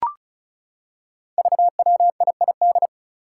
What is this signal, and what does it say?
Morse code, heard as a keyed tone, about 36 wpm, 700 Hz. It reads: VWIID
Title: Five Challenging Letter-Number Combinations 35wpm